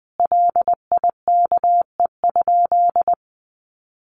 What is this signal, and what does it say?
Morse code, heard as a keyed tone, about 20 wpm, 700 Hz. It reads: LIKE?
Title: Sentences from Top 100 Words 20wpm